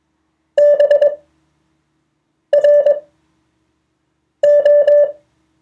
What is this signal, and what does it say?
Morse code, heard as a keyed tone, about 21 wpm, spaced as for 5 wpm, 580 Hz. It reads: BRO